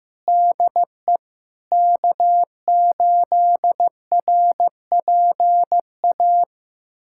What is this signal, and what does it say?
Morse code, heard as a keyed tone, about 15 wpm, 700 Hz. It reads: DE K8RPA
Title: USA Callsign DE Callsign - 5 Characters - 15wpm